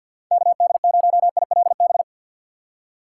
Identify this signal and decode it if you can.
Morse code, heard as a keyed tone, about 59 wpm, 700 Hz. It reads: KD0ILB
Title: Speed Racing - USA 2x3 Call Signs 50wpm